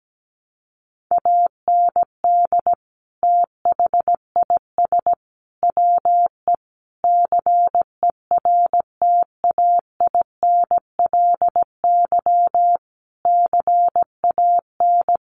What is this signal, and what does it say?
Morse code, heard as a keyed tone, about 17 wpm, 700 Hz. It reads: AND THIS WE CERTAINLY CAN
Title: Sentences from Top 900 Words 17wpm